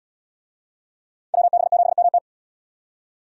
Morse code, heard as a keyed tone, about 74 wpm, 700 Hz. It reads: WB7MT